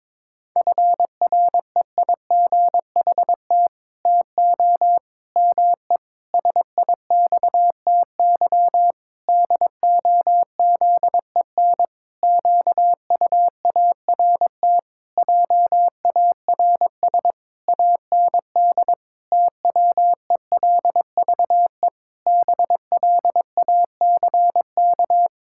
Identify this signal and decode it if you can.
Morse code, heard as a keyed tone, about 22 wpm, 700 Hz. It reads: FREIGHT TO ME SIXTY DOZEN QUART JARS AND TWELVE BLACK
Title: Pangram Challenge v2 - 22wpm